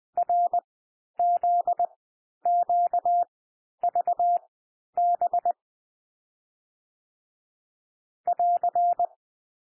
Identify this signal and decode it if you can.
Morse code, heard as a keyed tone, about 20 wpm, spaced as for 11 wpm, 700 Hz. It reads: RZQVB <AR>
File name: cwmp3 / M1 SOUND MIXED 20 wpm